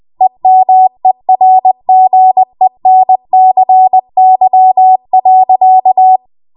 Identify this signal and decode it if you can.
Morse code, heard as a keyed tone, about 20 wpm, 750 Hz. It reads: EMERGENCY.
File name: Morse / 20WPM